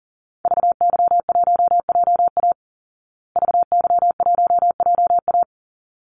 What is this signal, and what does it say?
Morse code, heard as a keyed tone, about 40 wpm, 700 Hz. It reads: VY1JA VY1JA
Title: Call Signs Repeated 3x - 40wpm